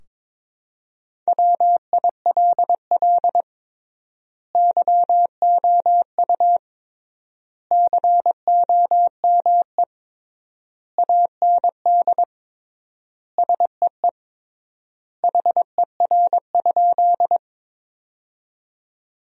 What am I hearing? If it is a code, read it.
Morse code, heard as a keyed tone, about 22 wpm, 700 Hz. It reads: WILL YOU COME AND SEE HER?
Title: 3x Word-Spacing - Sentences from Top 100 Words - 22wpm